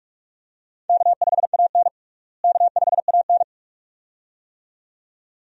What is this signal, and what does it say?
Morse code, heard as a keyed tone, about 42 wpm, 700 Hz. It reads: K5AN K5AN
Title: USA Call Signs - 4 Characters - Repeated 2x - 45wpm